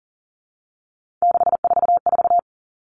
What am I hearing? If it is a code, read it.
Morse code, heard as a keyed tone, about 40 wpm, 700 Hz. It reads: 644
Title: All Cut Numbers - 3-Digit 40wpm